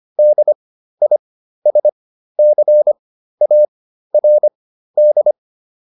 Morse code, heard as a keyed tone, about 25 wpm, spaced as for 13 wpm, 600 Hz. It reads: DISCARD
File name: DIS7 13